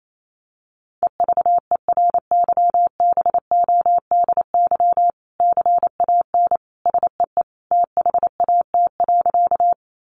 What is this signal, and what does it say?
Morse code, heard as a keyed tone, about 28 wpm, 700 Hz. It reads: EVERYBODY CAN SEE THAT.
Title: Sentences from Top 1400 Words 28wpm